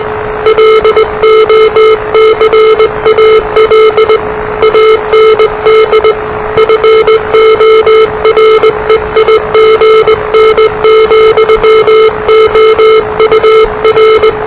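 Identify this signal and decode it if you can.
Morse code, heard as a keyed tone, about 19 wpm, 440 Hz, with noise. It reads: LOCAL AND FOREIGN, OUR
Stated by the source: Morse challenge from NZ Net, the national amateur radio net of Aotearoa/New Zealand, which meets on 3535.0 kHz.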